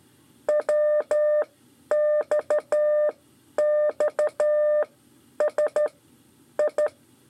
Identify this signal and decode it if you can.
Morse code, heard as a keyed tone, about 12 wpm, spaced as for 9 wpm, 600 Hz. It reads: WXXSI
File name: 5 LEZIONE RIPASSO 2